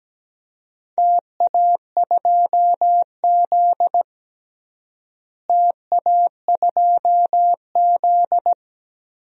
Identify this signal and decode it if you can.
Morse code, heard as a keyed tone, about 17 wpm, 700 Hz. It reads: TA2Z TA2Z
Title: DXCC-100 Call Signs - 4 Characters - Repeated 3x - 17wpm